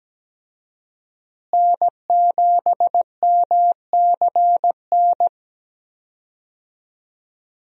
Morse code, heard as a keyed tone, about 17 wpm, 700 Hz. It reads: N7MCN
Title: USA Call Signs - 5 Characters - Repeated 2x - 17wpm